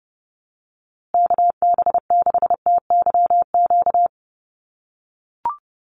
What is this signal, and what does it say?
Morse code, heard as a keyed tone, about 30 wpm, 700 Hz. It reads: KB6TYQ
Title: USA Call Signs Repeated 2x 30wpm